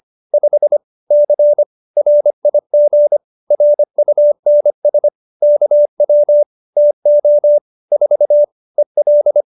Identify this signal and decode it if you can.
Morse code, heard as a keyed tone, about 25 wpm, 590 Hz. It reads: 5 C RIG RUNS KW TO 4 EL